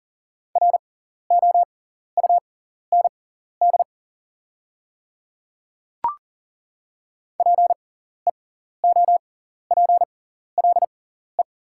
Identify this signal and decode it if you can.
Morse code, heard as a keyed tone, about 40 wpm, spaced as for 14 wpm, 700 Hz. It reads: ROUND PEOPLE
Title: 6x Character Spacing - Top 200 Words - 40wpm